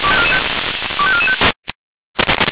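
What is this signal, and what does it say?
Five-tone selective call (ZVEI), heard as quick runs of five tones, tones of about 70 ms. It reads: 25445
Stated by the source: ZVEI five-tone selective call received with an SDR.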